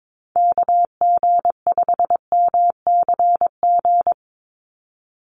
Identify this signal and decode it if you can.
Morse code, heard as a keyed tone, about 22 wpm, 700 Hz. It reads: KG5MCG
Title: USA Call Signs 22wpm